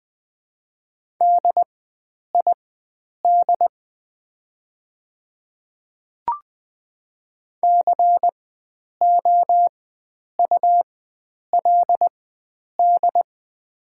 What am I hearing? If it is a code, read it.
Morse code, heard as a keyed tone, about 20 wpm, spaced as for 9 wpm, 700 Hz. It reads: DID COULD
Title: 4x Character Spacing - Top 100 Words - 20wpm